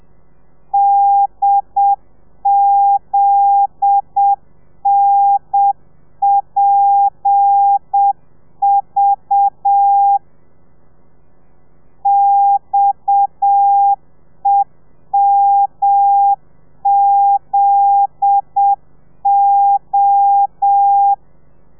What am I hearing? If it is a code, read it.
Morse code, heard as a keyed tone, about 7 wpm, 800 Hz. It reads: DZNPV XEMZO